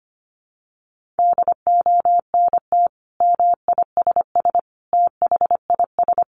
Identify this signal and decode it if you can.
Morse code, heard as a keyed tone, about 25 wpm, 700 Hz. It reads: DONT MISS THIS
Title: Sentences from Top 800 Words 25wpm